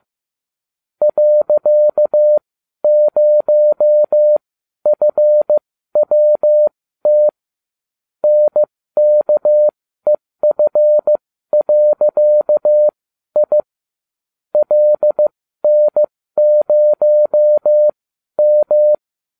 Morse code, heard as a keyed tone, about 15 wpm, spaced as for 12 wpm, 600 Hz. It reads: .0FWT NKEF.I LN0M